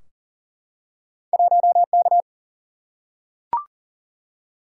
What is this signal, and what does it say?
Morse code, heard as a keyed tone, about 38 wpm, 700 Hz. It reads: 1K
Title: Single-Letter Single-Number 40wpm